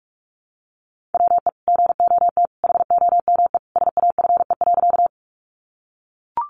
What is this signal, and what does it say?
Morse code, heard as a keyed tone, about 44 wpm, 700 Hz. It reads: WE GOT HOME SAFE.
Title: Sentences from Top 900 Words 45wpm